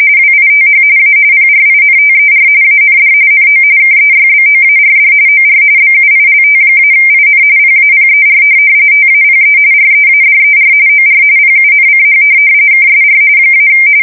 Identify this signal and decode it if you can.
RTTY, about 45.45 baud. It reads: WITH A PRINTER AND NO KEYBOARD OR BUILT-IN PAPER TAPE READER OR PUNCH KSR -
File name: fsk170 / teletype-corp